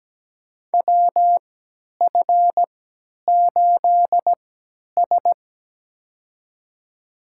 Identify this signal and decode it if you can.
Morse code, heard as a keyed tone, about 17 wpm, spaced as for 10 wpm, 700 Hz. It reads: WF8S
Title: USA Call Signs - 4 Characters - 3x Character Spacing - 17wpm